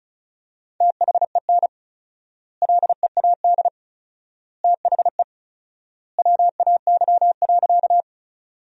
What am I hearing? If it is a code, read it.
Morse code, heard as a keyed tone, about 34 wpm, 700 Hz. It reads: THEN LEAD THE WAY.
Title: 4x Word-Spacing - Sentences from Top 500 Words - 35wpm